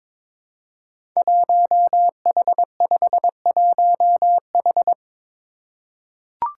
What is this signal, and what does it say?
Morse code, heard as a keyed tone, about 22 wpm, 700 Hz. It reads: 1H51H
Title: Five Challenging Letter-Number Combinations 22wpm